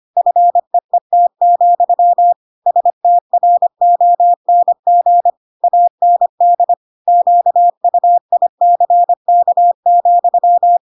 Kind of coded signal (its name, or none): Morse code